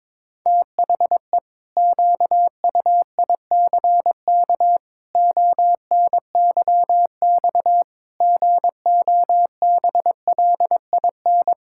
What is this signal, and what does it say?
Morse code, heard as a keyed tone, about 22 wpm, 700 Hz. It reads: THE QUICK ONYX GOBLIN
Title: Pangram Challenge v2 - 22wpm